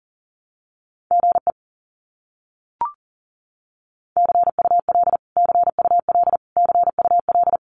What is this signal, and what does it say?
Morse code, heard as a keyed tone, about 40 wpm, 700 Hz. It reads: GE CUL CUL CUL
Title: Abbreviations Repeated 3x - 40wpm